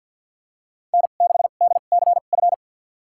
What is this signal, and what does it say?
Morse code, heard as a keyed tone, about 53 wpm, spaced as for 37 wpm, 700 Hz. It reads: N6DCF
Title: USA Call Signs - 5 Characters - 2x Character Spacing - 50wpm